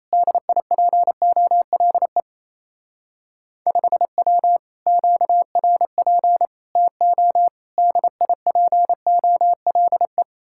Morse code, heard as a keyed tone, about 35 wpm, 700 Hz. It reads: DIPOLE 5W QRP TO DIPOLE